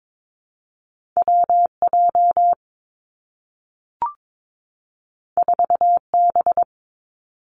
Morse code, heard as a keyed tone, about 22 wpm, 700 Hz. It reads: W J 4 B
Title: Two Challenging Letter-Number Combinations 22wpm